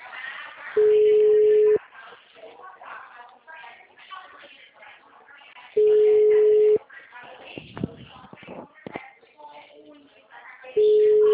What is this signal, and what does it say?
Ringback tone (Europe, 425 Hz), with speech.
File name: out- -2010- - - .53427